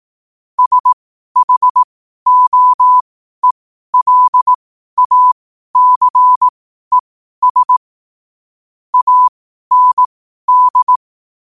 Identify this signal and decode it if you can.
Morse code, heard as a keyed tone, about 18 wpm, spaced as for 12 wpm, 1000 Hz. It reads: SHOELACES AND
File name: qotdfiles / 12wpm